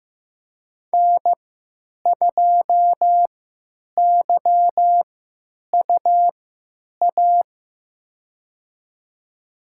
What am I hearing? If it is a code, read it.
Morse code, heard as a keyed tone, about 15 wpm, spaced as for 9 wpm, 700 Hz. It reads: N2YUA
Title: USA Call Signs - 5 Characters - 3x Character Spacing - 15wpm